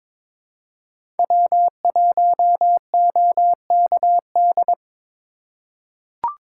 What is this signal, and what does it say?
Morse code, heard as a keyed tone, about 22 wpm, 700 Hz. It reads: W1OKD
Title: USA Call Signs - 5 Characters - 4x Character Spacing - 22wpm